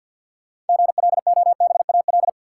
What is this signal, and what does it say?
Morse code, heard as a keyed tone, about 53 wpm, 700 Hz. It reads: GLOBAL
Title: Sentences from Top 1200 Words - New Word Drill Alt - 50wpm